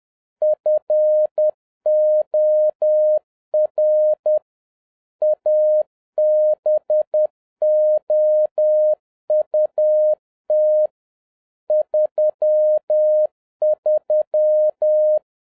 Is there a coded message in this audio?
Yes: Morse code from 0.4 to 15.2 s, heard as a keyed tone, about 10 wpm, 610 Hz. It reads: FOR ABOUT 33